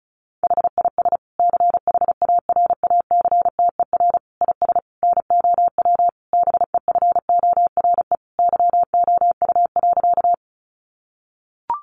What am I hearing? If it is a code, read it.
Morse code, heard as a keyed tone, about 34 wpm, 700 Hz. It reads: HIS CHARACTER IS NOW BEFORE YOU.
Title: Sentences from Top 1200 Words 35wpm